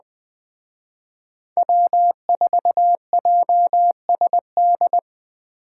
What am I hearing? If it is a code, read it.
Morse code, heard as a keyed tone, about 21 wpm, 700 Hz. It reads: W4JSD
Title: Speed Racing - USA 1x3 Call Signs 17wpm